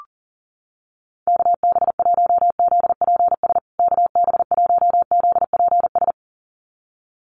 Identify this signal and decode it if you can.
Morse code, heard as a keyed tone, about 40 wpm, 700 Hz. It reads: KB1ZPS KB1ZPS
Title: USA Call Signs Repeated 2x 40wpm